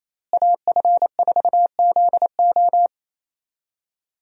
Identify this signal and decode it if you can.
Morse code, heard as a keyed tone, about 27 wpm, 700 Hz. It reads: AF4ZO
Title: USA Call Signs - 5 Characters - Repeated 3x - 28wpm